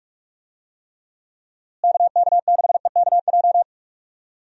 Morse code, heard as a keyed tone, about 41 wpm, 700 Hz. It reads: KK6EKJ
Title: USA Call Signs - 6 Characters - Repeated 3x - 45wpm